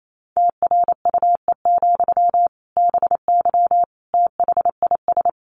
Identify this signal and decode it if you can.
Morse code, heard as a keyed tone, about 28 wpm, 700 Hz. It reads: TRUE, BY THIS
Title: Sentences from Top 1500 Words 28wpm